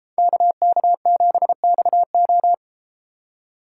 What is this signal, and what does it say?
Morse code, heard as a keyed tone, about 33 wpm, 700 Hz. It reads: KK7XO